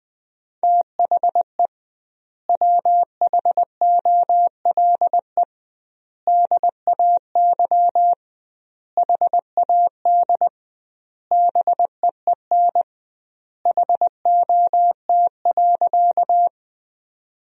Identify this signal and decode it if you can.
Morse code, heard as a keyed tone, about 21 wpm, 700 Hz. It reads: THE WHOLE DAY HAD BEEN HOT.
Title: 2x Word-Spacing - Sentences from Top 500 Words - 20wpm